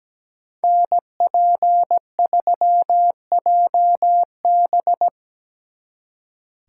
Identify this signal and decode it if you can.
Morse code, heard as a keyed tone, about 20 wpm, 700 Hz. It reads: NP3JB
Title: Speed Racing - USA Call Signs 17wpm